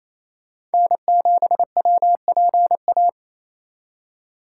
Morse code, heard as a keyed tone, about 36 wpm, 700 Hz. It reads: N7WPA